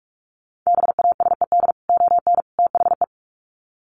Morse code, heard as a keyed tone, about 44 wpm, 700 Hz. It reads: BASED ON THE
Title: Sets of 3 Words 45wpm